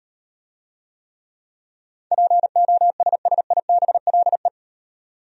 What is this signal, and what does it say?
Morse code, heard as a keyed tone, about 38 wpm, 700 Hz. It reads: POSSIBLE